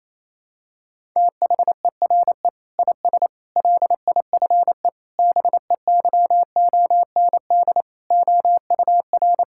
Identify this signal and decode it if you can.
Morse code, heard as a keyed tone, about 28 wpm, 700 Hz. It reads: THERE IS LIFE BEYOND OUR
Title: Sentences from Top 1000 Words - Review Part 6 - 28wpm